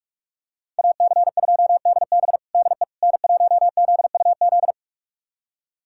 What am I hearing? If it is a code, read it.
Morse code, heard as a keyed tone, about 46 wpm, 700 Hz. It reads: AC2DB DE N1ZUZ
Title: USA Callsign DE Callsign - 5 Characters - 45wpm